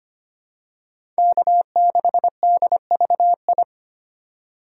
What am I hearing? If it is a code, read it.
Morse code, heard as a keyed tone, about 25 wpm, 700 Hz. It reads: K6DVI
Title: USA Call Signs - 5 Characters - 4x Character Spacing - 25wpm